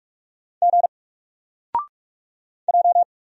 Morse code, heard as a keyed tone, about 44 wpm, 700 Hz. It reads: G J
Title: Single Letter-Number 45wpm